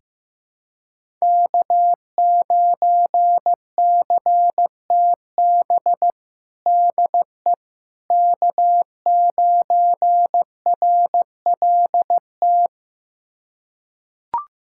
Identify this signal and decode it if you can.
Morse code, heard as a keyed tone, about 15 wpm, 700 Hz. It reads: K9CTB DE K9RLT
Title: USA Callsign DE Callsign - 5 Characters - 15wpm